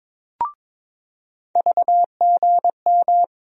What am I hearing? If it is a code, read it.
Morse code, heard as a keyed tone, about 22 wpm, 700 Hz. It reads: VGM
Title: USA Call Sign Postfixes - 3 Letters 22wpm